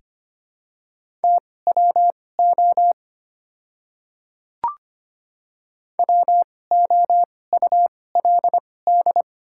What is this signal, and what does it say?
Morse code, heard as a keyed tone, about 25 wpm, spaced as for 18 wpm, 700 Hz. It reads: TWO WOULD
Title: 2x Character Spacing - Top 100 Words - 25wpm